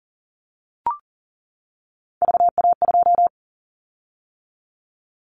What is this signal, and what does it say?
Morse code, heard as a keyed tone, about 42 wpm, 700 Hz. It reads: VA2
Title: Callsign Prefixes for Top 25 DXCC - 40wpm